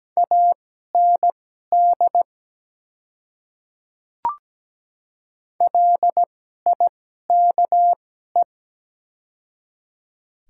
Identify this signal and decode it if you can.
Morse code, heard as a keyed tone, about 17 wpm, spaced as for 12 wpm, 700 Hz. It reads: AND LIKE
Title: 2x Character Spacing - Top 100 Words - 17wpm